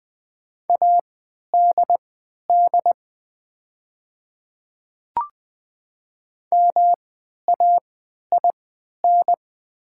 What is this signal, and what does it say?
Morse code, heard as a keyed tone, about 20 wpm, spaced as for 11 wpm, 700 Hz. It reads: ADD MAIN
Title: 3x Character Spacing - Top 300 Words - 20wpm